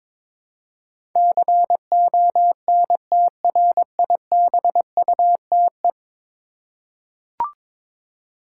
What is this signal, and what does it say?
Morse code, heard as a keyed tone, about 22 wpm, 700 Hz. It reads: CONTRIBUTE
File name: practice / 22wpm / 10 Letter Words - First 3 Letters Given - 22wpm